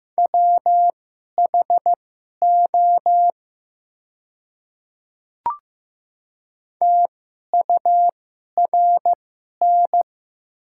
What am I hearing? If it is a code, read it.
Morse code, heard as a keyed tone, about 16 wpm, spaced as for 11 wpm, 700 Hz. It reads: WHO TURN
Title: 2x Character Spacing - Top 200 Words - 15wpm